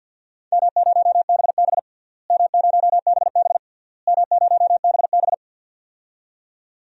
Morse code, heard as a keyed tone, about 46 wpm, 700 Hz. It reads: M0BB M0BB M0BB
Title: DXCC-100 Call Signs - 4 Characters - Repeated 3x - 50wpm